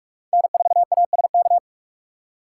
Morse code, heard as a keyed tone, about 45 wpm, 700 Hz. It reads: N4ASK
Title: USA Call Signs - 3x Character Spacing - 45wpm